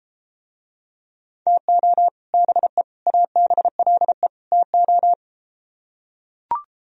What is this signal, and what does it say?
Morse code, heard as a keyed tone, about 33 wpm, 700 Hz. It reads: TO BE ABLE TO